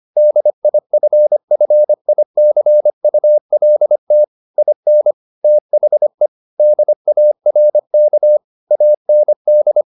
Morse code, heard as a keyed tone, about 25 wpm, 600 Hz. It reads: DIFFICULT IN THE DARK AND
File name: the old man and the sea part 1 25wpm